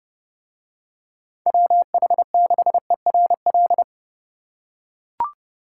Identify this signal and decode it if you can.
Morse code, heard as a keyed tone, about 31 wpm, 700 Hz. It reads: WH6ERL